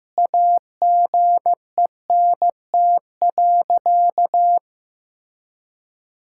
Morse code, heard as a keyed tone, about 15 wpm, 700 Hz. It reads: AGENT.
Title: Sentences from Top 1600 Words 15wpm